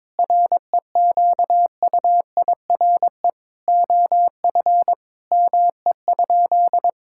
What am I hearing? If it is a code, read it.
Morse code, heard as a keyed tone, about 22 wpm, 700 Hz. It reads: REQUIRE OF ME?
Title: Sentences from Top 1000 Words - Review Part 6 - 22wpm